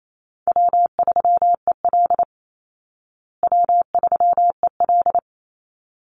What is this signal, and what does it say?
Morse code, heard as a keyed tone, about 28 wpm, 700 Hz. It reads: W3EL W3EL
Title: Easy Call Signs Repeated 2x - 28wpm